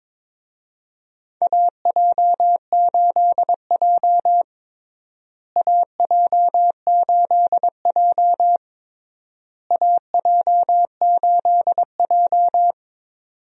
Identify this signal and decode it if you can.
Morse code, heard as a keyed tone, about 22 wpm, 700 Hz. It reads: AJ8J AJ8J AJ8J
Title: USA Call Signs - 4 Characters - Repeated 3x - 22wpm